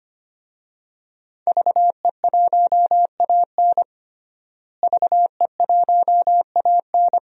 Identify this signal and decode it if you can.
Morse code, heard as a keyed tone, about 25 wpm, 700 Hz. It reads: VE1AN VE1AN
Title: DXCC-100 Call Signs - 5 Characters - Repeated 2x - 25wpm